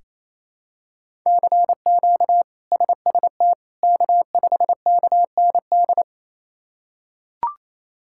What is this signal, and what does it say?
Morse code, heard as a keyed tone, about 28 wpm, 700 Hz. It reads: CQ SST K5KND
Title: K1USN SST QSO Practice - 28wpm